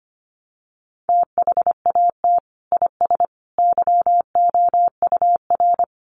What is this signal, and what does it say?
Morse code, heard as a keyed tone, about 25 wpm, 700 Hz. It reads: THAT IS YOUR
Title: Sentences from Top 1200 Words 25wpm